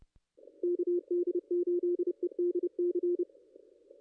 Morse code, heard as a keyed tone, about 30 wpm, 350 Hz. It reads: KD8EDC
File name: contests / ss-cw-k5tr-op-n5rz -2200